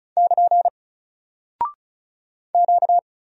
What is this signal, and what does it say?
Morse code, heard as a keyed tone, about 36 wpm, 700 Hz. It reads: <KN> Q
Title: Single Character (Letter:Number:Punctuation:Prosign) Alt 35wpm